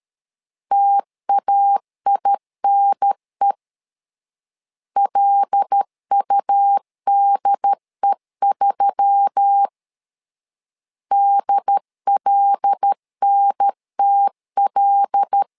Morse code, heard as a keyed tone, about 13 wpm, 800 Hz. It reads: TAINE LUDE3 DLNTL
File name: LEZIONI / 5 LBVC / 5 L 12wpm 1913